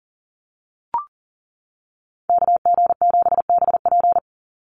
Morse code, heard as a keyed tone, about 40 wpm, 700 Hz. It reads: KG7BP
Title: USA Call Signs 40wpm